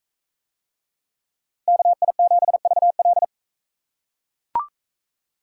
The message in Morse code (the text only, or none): KI7VL